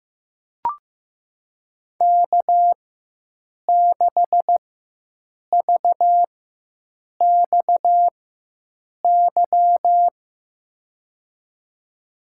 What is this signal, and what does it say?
Morse code, heard as a keyed tone, about 15 wpm, spaced as for 7 wpm, 700 Hz. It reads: K6VXY
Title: USA Call Signs - 5 Characters - 4x Character Spacing - 15wpm